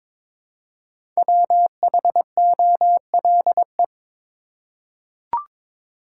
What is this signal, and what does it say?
Morse code, heard as a keyed tone, about 22 wpm, 700 Hz. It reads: WHOLE
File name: 5 Letter Words - First 3 Letters Given - 22wpm